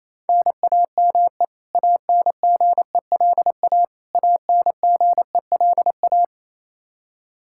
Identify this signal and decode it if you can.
Morse code, heard as a keyed tone, about 28 wpm, 700 Hz. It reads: NAME ANGELA ANGELA
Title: Practice QSO - Exchange 1 - Beginning - 28wpm